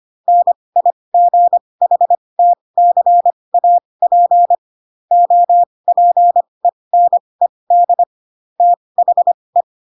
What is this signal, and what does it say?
Morse code, heard as a keyed tone, about 25 wpm, spaced as for 20 wpm, 700 Hz. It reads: NIGHTCAP OPENED THE
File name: Pinocchio / Pinocchio ch6 15 25 ik1hge